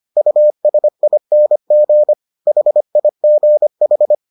Morse code, heard as a keyed tone, about 25 wpm, 600 Hz. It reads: USING HIGH